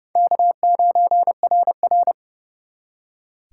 Morse code, heard as a keyed tone, about 37 wpm, 700 Hz. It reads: K9RR
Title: Speed Racing - USA Call Signs 30wpm